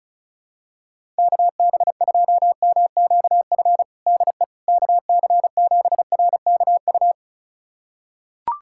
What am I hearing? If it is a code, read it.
Morse code, heard as a keyed tone, about 35 wpm, 700 Hz. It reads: KB2MQF DE KC7RKU
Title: USA Callsign DE Callsign 35wpm